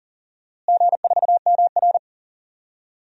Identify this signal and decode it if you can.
Morse code, heard as a keyed tone, about 40 wpm, 700 Hz. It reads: G4MR
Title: DXCC-100 Call Signs - 4 Characters - Repeated 2x - 40wpm